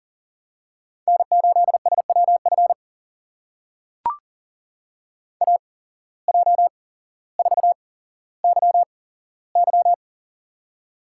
Morse code, heard as a keyed tone, about 42 wpm, spaced as for 17 wpm, 700 Hz. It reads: N8SWF AJ4YY
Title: USA Call Signs - 5 Characters - 8x Character Spacing - 40wpm